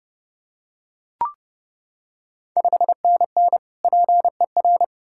Morse code, heard as a keyed tone, about 30 wpm, 700 Hz. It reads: 5NN PER
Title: Ontario QSO Party Exchange 30wpm